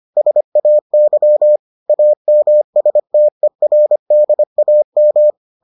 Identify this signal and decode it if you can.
Morse code, heard as a keyed tone, about 25 wpm, 600 Hz. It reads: SAY AMSTERDAM